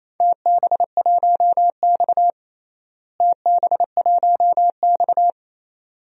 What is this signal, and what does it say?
Morse code, heard as a keyed tone, about 28 wpm, 700 Hz. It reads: TB1X TB1X
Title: DXCC-100 Call Signs - 4 Characters - Repeated 2x - 28wpm